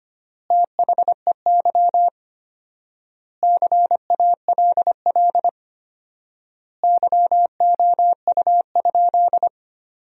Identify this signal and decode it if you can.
Morse code, heard as a keyed tone, about 25 wpm, 700 Hz. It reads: THEY CALL YOU?
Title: 4x Word-Spacing - Sentences from Top 100 Words - 25wpm